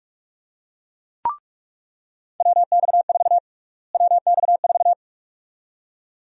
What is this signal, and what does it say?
Morse code, heard as a keyed tone, about 46 wpm, 700 Hz. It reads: WX4 WX4
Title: USA Call Sign Prefixes - Repeated 2x - 45wpm